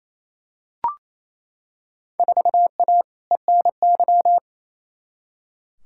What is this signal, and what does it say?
Morse code, heard as a keyed tone, about 28 wpm, 700 Hz. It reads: 4A ENY
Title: Field Day Exchanges Encore - 28wpm